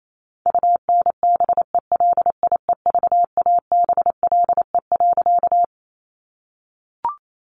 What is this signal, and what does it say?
Morse code, heard as a keyed tone, about 28 wpm, 700 Hz. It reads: UNBELIEVABLE.
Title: A Sound Of Thunder 28wpm